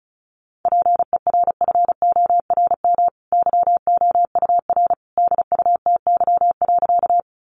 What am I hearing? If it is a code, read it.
Morse code, heard as a keyed tone, about 34 wpm, 700 Hz. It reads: PERFORM YOUR DUTY.
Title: Sentences from Top 1500 Words 35wpm